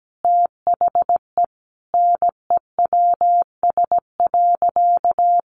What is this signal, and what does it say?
Morse code, heard as a keyed tone, about 17 wpm, 700 Hz. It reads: THE NEWS.
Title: Sentences from Top 800 Words 17wpm